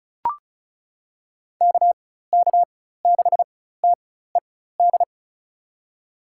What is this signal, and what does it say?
Morse code, heard as a keyed tone, about 36 wpm, spaced as for 16 wpm, 700 Hz. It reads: KK6TED
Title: USA Call Signs - 6 Characters - 4x Character Spacing - 35wpm